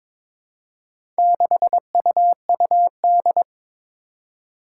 Morse code, heard as a keyed tone, about 23 wpm, 700 Hz. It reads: 6UUD
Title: Four Challenging Letter-Number Combinations 22wpm